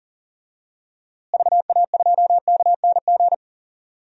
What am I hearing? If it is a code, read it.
Morse code, heard as a keyed tone, about 40 wpm, 700 Hz. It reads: VA2KNG
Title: DXCC-100 Call Signs - 6 Characters - Repeated 2x - 40wpm